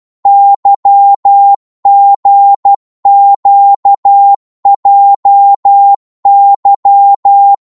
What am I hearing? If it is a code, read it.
Morse code, heard as a keyed tone, about 12 wpm, 800 Hz. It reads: YGQJY
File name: Morse / ltr-12wpm